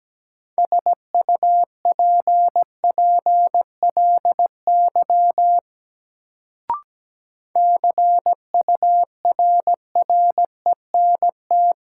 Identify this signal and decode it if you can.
Morse code, heard as a keyed tone, about 17 wpm, 700 Hz. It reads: SUPPLY CURRENT
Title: Sentences from Top 1000 Words - Review Part 4 - New Word Drill - 17wpm